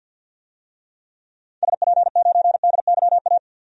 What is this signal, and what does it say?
Morse code, heard as a keyed tone, about 50 wpm, 700 Hz. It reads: SP9DYA
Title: DXCC-100 Call Signs - 6 Characters - 50wpm